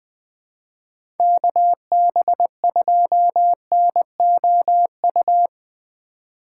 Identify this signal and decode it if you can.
Morse code, heard as a keyed tone, about 20 wpm, 700 Hz. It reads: KB2NOU
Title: USA Call Signs - 6 Characters - 4x Character Spacing - 20wpm